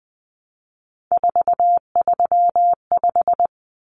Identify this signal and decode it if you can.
Morse code, heard as a keyed tone, about 20 wpm, 700 Hz. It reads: 435
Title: Common Cut Numbers - 3-Digit 20wpm